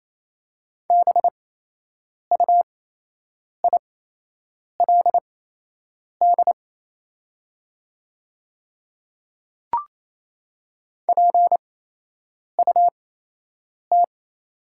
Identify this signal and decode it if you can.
Morse code, heard as a keyed tone, about 28 wpm, spaced as for 8 wpm, 700 Hz. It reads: BUILD PUT
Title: 8x Character Spacing - Top 300 Words - 28wpm